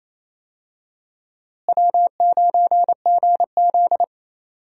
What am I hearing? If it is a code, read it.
Morse code, heard as a keyed tone, about 28 wpm, 700 Hz. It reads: W9GZ
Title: USA Call Signs - 4 Characters - Repeated 3x - 28wpm